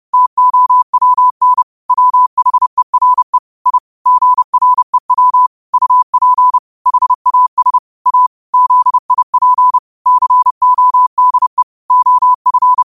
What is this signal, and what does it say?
Morse code, heard as a keyed tone, about 30 wpm, 1000 Hz. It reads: TOWN WHERE I GREW UP HAS A ZIP CODE OF